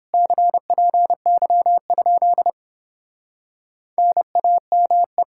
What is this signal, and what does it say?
Morse code, heard as a keyed tone, about 29 wpm, 700 Hz. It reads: CPY? NAME